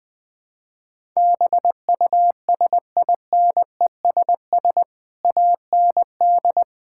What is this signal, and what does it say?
Morse code, heard as a keyed tone, about 20 wpm, 700 Hz. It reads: BUSINESS AND
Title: Sentences from Top 1000 Words - Review Part 5 - 20wpm